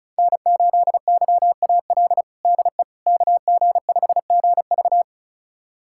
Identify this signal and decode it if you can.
Morse code, heard as a keyed tone, about 34 wpm, 700 Hz. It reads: N8YAL DE KG5GV
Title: USA Callsign DE Callsign - 5 Characters - 35wpm